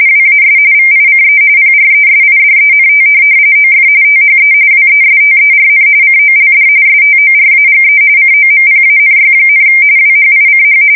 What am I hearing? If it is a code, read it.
RTTY, about 45.45 baud. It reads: WAS VICE PRESIDENT OF THE WESTERN COLD STORAGE COMPANY, WHICH
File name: fsk170 / teletype-corp